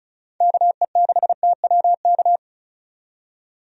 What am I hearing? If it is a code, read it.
Morse code, heard as a keyed tone, about 36 wpm, 700 Hz. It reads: KE6TWK
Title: USA Call Signs - 6 Characters - 35wpm